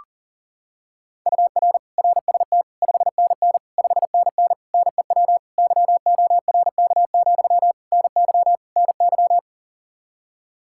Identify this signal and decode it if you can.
Morse code, heard as a keyed tone, about 40 wpm, 700 Hz. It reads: UR RST 5NN 5NN NEW YORK, NY NY
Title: Practice QSO - Exchange 1 - Beginning - 40wpm